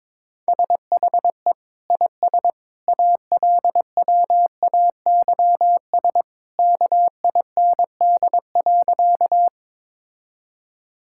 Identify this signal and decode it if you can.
Morse code, heard as a keyed tone, about 22 wpm, 700 Hz. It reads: SHE IS ALWAYS KIND.
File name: practice / 22wpm / Sentences with 2-Word Phrases - 22wpm